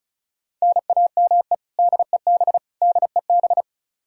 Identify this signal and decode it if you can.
Morse code, heard as a keyed tone, about 36 wpm, 700 Hz. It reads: NAME DEB DEB
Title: Practice QSO - Exchange 1 - Beginning - 35wpm